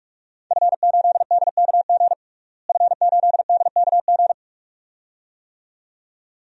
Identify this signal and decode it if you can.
Morse code, heard as a keyed tone, about 46 wpm, 700 Hz. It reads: F8DKG F8DKG
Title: DXCC-100 Call Signs - 5 Characters - Repeated 2x - 45wpm